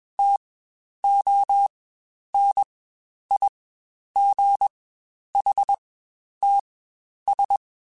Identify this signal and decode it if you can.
Morse code, heard as a keyed tone, about 21 wpm, spaced as for 10 wpm, 800 Hz. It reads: TONIGHTS
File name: Spaced Characters / Spaced Characters-mp3s / Weather5 5 sp @20